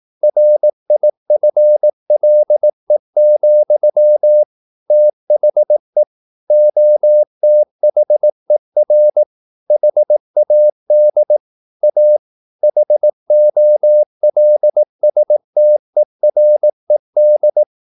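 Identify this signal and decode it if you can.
Morse code, heard as a keyed tone, about 18 wpm, 600 Hz. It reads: RIFLE, THE OTHER HAD A HOLSTERED
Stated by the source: Morse practice at 18 wpm, English text, from CW med Gnister.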